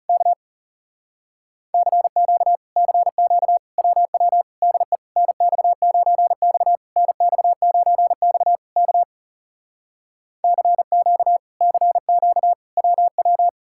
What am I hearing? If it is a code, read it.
Morse code, heard as a keyed tone, about 38 wpm, 700 Hz. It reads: K CQ CQ WW DE NX9X NX9X K CQ CQ WW
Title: Speed-Racing Common QSO Phrases - 30wpm